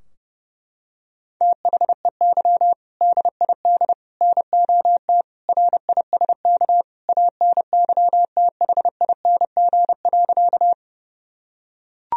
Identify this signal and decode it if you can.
Morse code, heard as a keyed tone, about 28 wpm, 700 Hz. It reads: THEY DID NOT RISK ANYTHING.
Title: Sentences from Top 1000 Words - Review Part 4 - 30wpm